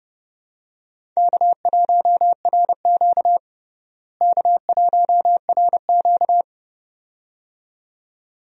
Morse code, heard as a keyed tone, about 32 wpm, 700 Hz. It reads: K1RQ K1RQ
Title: USA Call Signs - 4 Characters - Repeated 2x - 30wpm